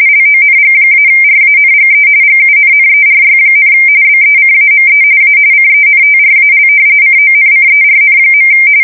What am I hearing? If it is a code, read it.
RTTY, about 45.45 baud. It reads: LONG-RANGE GUNNERY, BECAME A CENTRAL CONCERN.
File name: fsk170 / usnavy3